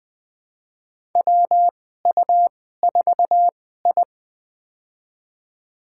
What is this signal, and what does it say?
Morse code, heard as a keyed tone, about 21 wpm, spaced as for 15 wpm, 700 Hz. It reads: WU4I
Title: USA Call Signs - 2x Character Spacing - 20wpm